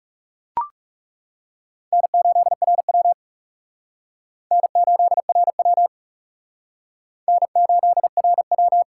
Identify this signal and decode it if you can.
Morse code, heard as a keyed tone, about 37 wpm, 700 Hz. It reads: N8RW N8RW N8RW
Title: Speed Racing - USA Call Signs 30wpm